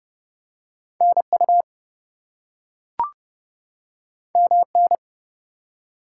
Morse code, heard as a keyed tone, about 30 wpm, 700 Hz. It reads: NU MN
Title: US States and Canadian Provinces 30wpm